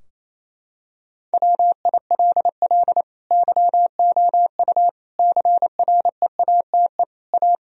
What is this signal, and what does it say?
Morse code, heard as a keyed tone, about 28 wpm, 700 Hz. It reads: WILL YOU CREATE A
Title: Sentences from Top 1000 Words - Review Part 3 - 28wpm